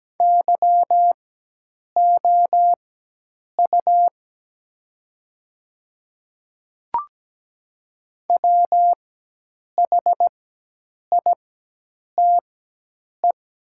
Morse code, heard as a keyed tone, about 17 wpm, spaced as for 8 wpm, 700 Hz. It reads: YOU WHITE
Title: 4x Character Spacing - Top 300 Words - 17wpm